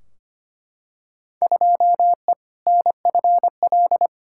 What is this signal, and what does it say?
Morse code, heard as a keyed tone, about 25 wpm, 700 Hz. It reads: 2E NFL
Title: Field Day Exchanges Encore - 25wpm